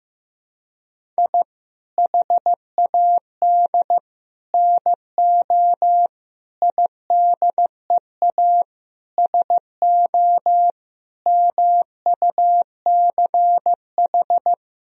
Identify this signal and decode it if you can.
Morse code, heard as a keyed tone, about 15 wpm, 700 Hz. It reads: I HAD NO IDEA SO MUCH
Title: Sentences from Top 1000 Words - Review Part 4 - 15wpm